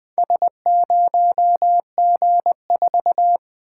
Morse code, heard as a keyed tone, about 20 wpm, 700 Hz. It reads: S0G4
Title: Four Letter-Number Combinations - 20wpm